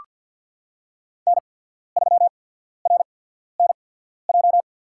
Morse code, heard as a keyed tone, about 54 wpm, spaced as for 14 wpm, 700 Hz. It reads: N3RNJ
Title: USA Call Signs - 5 Characters - 8x Character Spacing - 50wpm